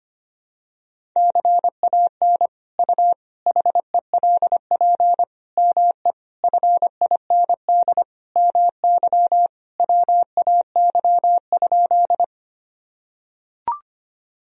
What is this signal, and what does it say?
Morse code, heard as a keyed tone, about 25 wpm, 700 Hz. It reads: CAN U HELP ME FIND MY WAY?
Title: Sentences from Top 200 Words and Abbreviations 25wpm